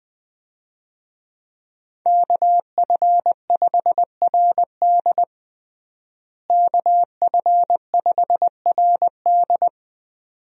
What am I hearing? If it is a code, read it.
Morse code, heard as a keyed tone, about 20 wpm, 700 Hz. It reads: KF5RD KF5RD
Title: USA Call Signs - 5 Characters - Repeated 2x - 20wpm